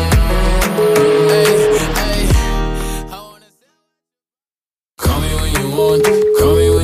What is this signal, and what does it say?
Ringback tone (Europe, 425 Hz), with noise.